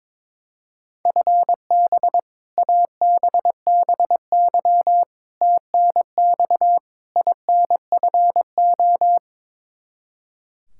Morse code, heard as a keyed tone, about 22 wpm, 700 Hz. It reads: FB ABBY TNX INFO
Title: Practice QSO - Exchange 3 - More Info - 22wpm